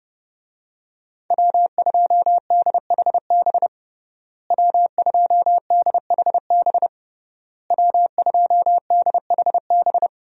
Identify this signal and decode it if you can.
Morse code, heard as a keyed tone, about 30 wpm, 700 Hz. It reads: W2DHB W2DHB W2DHB
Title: USA Call Signs - 5 Characters - Repeated 3x - 30wpm